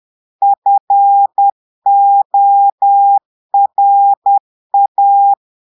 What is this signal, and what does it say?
Morse code, heard as a keyed tone, about 10 wpm, 800 Hz. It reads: FORA
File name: Portugais - texte en clair 1 - vitesse - 10 mots minute